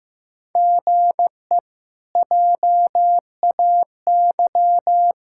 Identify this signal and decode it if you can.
Morse code, heard as a keyed tone, about 15 wpm, 700 Hz. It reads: GE JAY
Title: K1USN SST QSO Practice - 15wpm